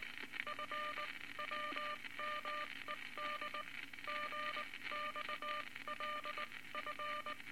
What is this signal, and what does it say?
Morse code, heard as a keyed tone, about 20 wpm, spaced as for 17 wpm, 1200 Hz, with noise. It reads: FWMED GXLF
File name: Morse Code Records / 19 B LCM 167 Twelve Words Per Minute